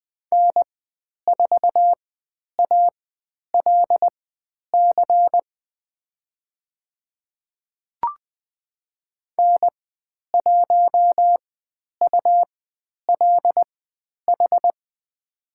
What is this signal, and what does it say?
Morse code, heard as a keyed tone, about 20 wpm, spaced as for 10 wpm, 700 Hz. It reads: N4ALC N1ULH
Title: 102 Lesson 17g - Letter u - Call Signs 20wpm